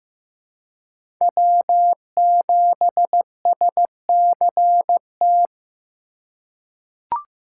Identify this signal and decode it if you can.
Morse code, heard as a keyed tone, about 15 wpm, 700 Hz. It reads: W7SCT
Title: USA Call Signs - 4x Character Spacing - 15wpm